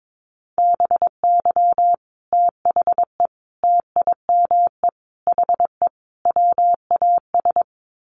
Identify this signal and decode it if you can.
Morse code, heard as a keyed tone, about 22 wpm, 700 Hz. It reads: BY THE TIME HE WAS
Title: Sets of 5 Words Encore 22wpm